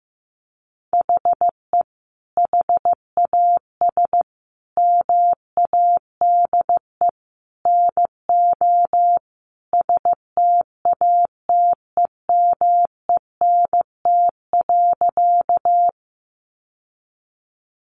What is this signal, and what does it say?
Morse code, heard as a keyed tone, about 14 wpm, 700 Hz. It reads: HE HAS MADE NO STATEMENT.
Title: Sentences from Top 1200 Words 15wpm